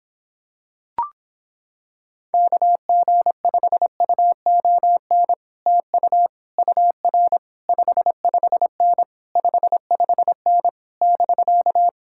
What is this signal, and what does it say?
Morse code, heard as a keyed tone, about 26 wpm, 700 Hz. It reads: KG5UON TU UR 55N 55N <BK>